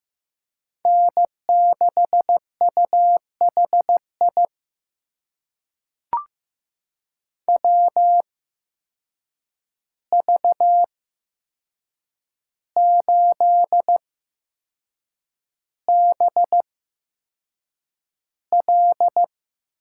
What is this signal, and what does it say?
Morse code, heard as a keyed tone, about 15 wpm, spaced as for 6 wpm, 700 Hz. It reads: N6UHI WV8BL
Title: USA Call Signs - 5 Characters - 8x Character Spacing - 15wpm